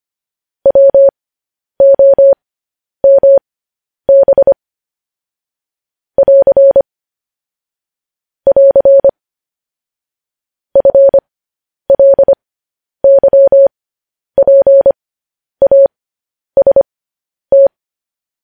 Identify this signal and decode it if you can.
Morse code, heard as a keyed tone, about 25 wpm, spaced as for 10 wpm, 550 Hz. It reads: WOMB <AR> <AR> FLYPAST
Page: news headlines in morse code at 10 wpm